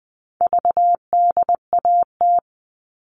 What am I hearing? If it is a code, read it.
Morse code, heard as a keyed tone, about 20 wpm, 700 Hz. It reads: VDAT
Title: All Cut Numbers - 4-Digit 20wpm